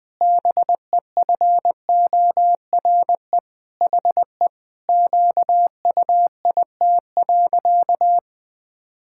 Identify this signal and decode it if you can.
Morse code, heard as a keyed tone, about 20 wpm, 700 Hz. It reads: BEFORE HE QUIT.
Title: Pangram Challenge v2 - 20wpm